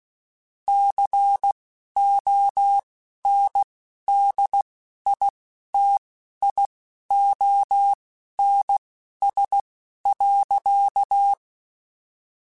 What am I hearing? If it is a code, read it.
Morse code, heard as a keyed tone, about 16 wpm, spaced as for 11 wpm, 800 Hz. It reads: CONDITIONS.
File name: Technical Subjects / Technical Subjects-mp3s / AND Gate @10